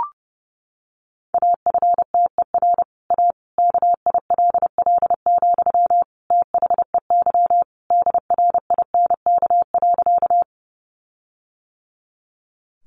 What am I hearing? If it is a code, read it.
Morse code, heard as a keyed tone, about 30 wpm, 700 Hz. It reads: AFTER A KILL, THEY DRINK.
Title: Sentences from Top 1500 Words 30wpm